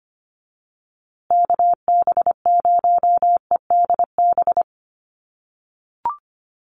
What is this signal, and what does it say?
Morse code, heard as a keyed tone, about 25 wpm, 700 Hz. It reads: KB0EDB
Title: USA Call Signs Repeated 3x 25wpm